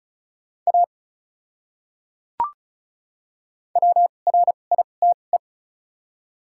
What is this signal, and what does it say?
Morse code, heard as a keyed tone, about 34 wpm, spaced as for 25 wpm, 700 Hz. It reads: A WRITE
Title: 2x Character Spacing - Top 100 Words - 35wpm